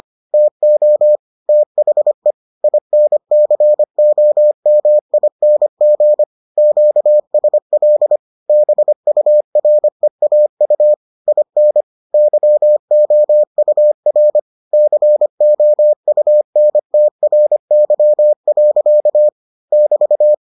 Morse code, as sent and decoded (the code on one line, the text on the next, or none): - --- / - .... / .. -. -.-. --- -- .. -. --. / --.- ... .-.. / -... ..- .-. .- ..- / .. -. / -.-- --- ..- .-. / -.-. --- ..- -. - .-. -.-- .-.-.- / -...-
TO THE INCOMING QSL BUREAU IN YOUR COUNTRY. <BT>